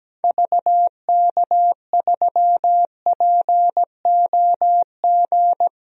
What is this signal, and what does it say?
Morse code, heard as a keyed tone, about 17 wpm, 700 Hz. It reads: VK3POG
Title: DXCC-100 Call Signs - 6 Characters - Repeated 2x - 17wpm